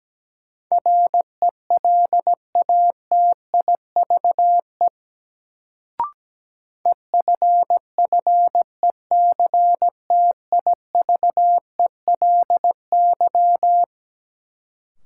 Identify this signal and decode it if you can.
Morse code, heard as a keyed tone, about 17 wpm, 700 Hz. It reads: RELATIVE EFFECTIVELY
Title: Sentences from Top 1400 Words - New Word Drill - 17wpm